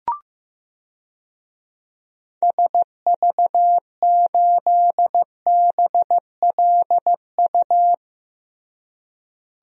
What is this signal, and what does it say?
Morse code, heard as a keyed tone, about 15 wpm, 700 Hz. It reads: SV8BLU
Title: DXCC-100 Call Signs - 6 Characters - Repeated 2x - 15wpm